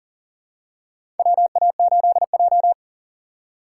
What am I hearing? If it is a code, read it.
Morse code, heard as a keyed tone, about 40 wpm, 700 Hz. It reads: WA8J
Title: USA Call Signs - 4 Characters - 2x Character Spacing - 40wpm